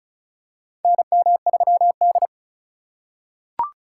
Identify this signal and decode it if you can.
Morse code, heard as a keyed tone, about 36 wpm, 700 Hz. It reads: NM3D
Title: USA Call Signs - 4 Characters - 3x Character Spacing - 35wpm